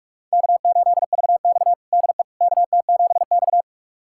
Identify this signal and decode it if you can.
Morse code, heard as a keyed tone, about 46 wpm, 700 Hz. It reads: K8VX DE KT7X
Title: USA Callsign DE Callsign - 4 Characters - 45wpm